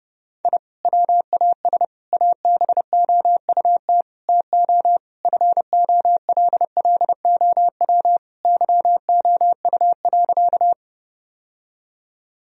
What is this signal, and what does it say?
Morse code, heard as a keyed tone, about 30 wpm, 700 Hz. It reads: I WAS ABOUT TO FOLLOW YOU.
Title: Sentences from Top 1000 Words - Review Part 1 - 30wpm